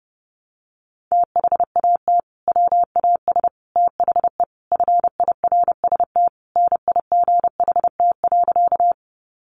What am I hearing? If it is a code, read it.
Morse code, heard as a keyed tone, about 30 wpm, 700 Hz. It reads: THAT WAS THE FIRST NIGHT.
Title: Sentences from Top 300 Words 30wpm